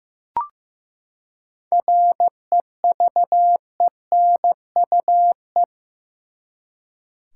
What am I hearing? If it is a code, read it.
Morse code, heard as a keyed tone, about 15 wpm, 700 Hz. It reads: REVENUE
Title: Seven Letter Words - 15wpm